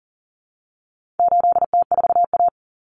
Morse code, heard as a keyed tone, about 40 wpm, 700 Hz. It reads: 8T4A
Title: Common Cut Numbers - 4-Digit 40wpm